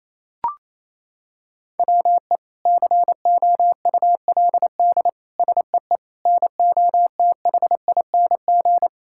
Morse code, heard as a keyed tone, about 28 wpm, 700 Hz. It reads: WE COULD SEE NOTHING
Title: Sentences from Top 1000 Words - Review Part 5 - 28wpm